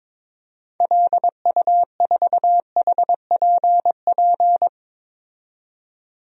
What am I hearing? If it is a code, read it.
Morse code, heard as a keyed tone, about 22 wpm, 700 Hz. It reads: LU4HPP
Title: DXCC-100 Call Signs - 6 Characters - 22wpm